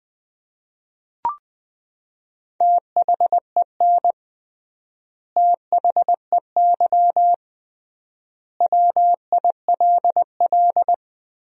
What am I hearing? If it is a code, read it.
Morse code, heard as a keyed tone, about 21 wpm, 700 Hz. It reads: THEN THEY WILL
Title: 3x Word-Spacing - Sentences from Top 200 Words - 20wpm